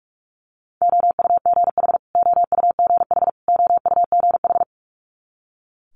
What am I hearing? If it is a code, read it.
Morse code, heard as a keyed tone, about 46 wpm, 700 Hz. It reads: OUGH OUGH OUGH
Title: Multi-letter Phonemes Repeated 3x 45wpm